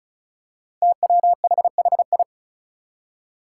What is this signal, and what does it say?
Morse code, heard as a keyed tone, about 36 wpm, 700 Hz. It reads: TWHHI
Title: Five Challenging Letter-Number Combinations 35wpm